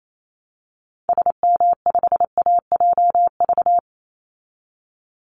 Morse code, heard as a keyed tone, about 28 wpm, 700 Hz. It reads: SM5AJV
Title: Call Signs Repeated 3x - 28wpm